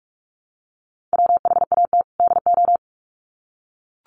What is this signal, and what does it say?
Morse code, heard as a keyed tone, about 44 wpm, 700 Hz. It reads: WHAT DO
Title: Sets of 2 Words 45wpm